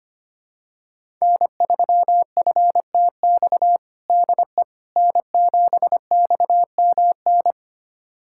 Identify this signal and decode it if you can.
Morse code, heard as a keyed tone, about 25 wpm, 700 Hz. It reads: N3FTX DE N7XMN
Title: USA Callsign DE Callsign - 5 Characters - 25wpm